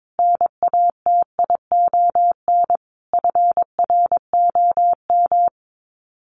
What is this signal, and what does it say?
Morse code, heard as a keyed tone, about 22 wpm, 700 Hz. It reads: NATION FROM
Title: Sets of 5 Words 22wpm